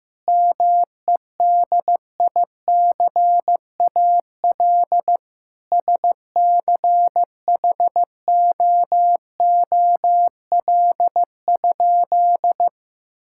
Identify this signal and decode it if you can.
Morse code, heard as a keyed tone, about 16 wpm, 700 Hz. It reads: MEDICAL SCHOOL?
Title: Sentences from Top 1000 Words - Review Part 5 - 15wpm